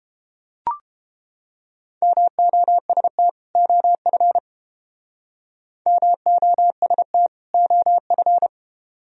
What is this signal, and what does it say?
Morse code, heard as a keyed tone, about 30 wpm, 700 Hz. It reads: MOST OF MOST OF